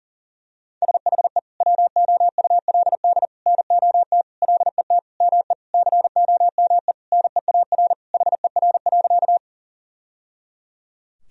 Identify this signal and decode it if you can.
Morse code, heard as a keyed tone, about 40 wpm, 700 Hz. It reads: SHE WOULD NOT LET ME COME NEAR HER.
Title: Sentences from Top 1000 Words - Review Part 1 - 40wpm